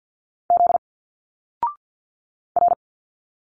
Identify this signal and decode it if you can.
Morse code, heard as a keyed tone, about 50 wpm, 700 Hz. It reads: Z R
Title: Single Letters 50wpm